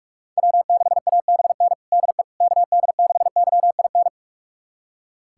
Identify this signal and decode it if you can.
Morse code, heard as a keyed tone, about 46 wpm, 700 Hz. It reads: W6ABN DE KD6YIN
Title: USA Callsign DE Callsign 45wpm